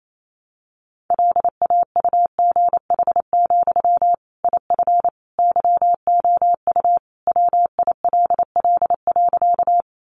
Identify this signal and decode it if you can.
Morse code, heard as a keyed tone, about 28 wpm, 700 Hz. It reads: LAUGH, IF YOU WILL.
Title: Sentences from Top 500 Words 28wpm